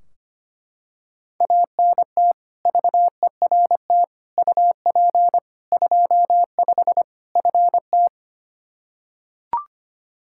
Morse code, heard as a keyed tone, about 25 wpm, 700 Hz. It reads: ANT VERT UP 25 FT
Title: K1USN SST QSO Practice 25wpm